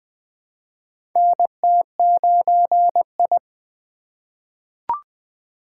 Morse code, heard as a keyed tone, about 20 wpm, 700 Hz. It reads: NT9I
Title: USA Call Signs - 2x Character Spacing - 20wpm